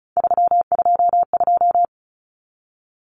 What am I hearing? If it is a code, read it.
Morse code, heard as a keyed tone, about 34 wpm, 700 Hz. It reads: 322
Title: Numbers - 3-Digit 35wpm